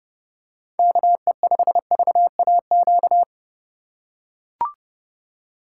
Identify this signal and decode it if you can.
Morse code, heard as a keyed tone, about 30 wpm, 700 Hz. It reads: KE5VAQ
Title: USA Call Signs - 6 Characters - 3x Character Spacing - 30wpm